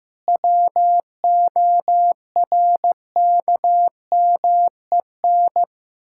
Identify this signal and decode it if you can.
Morse code, heard as a keyed tone, about 15 wpm, 700 Hz. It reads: WORKMEN
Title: Pangram Challenge v2 - 15wpm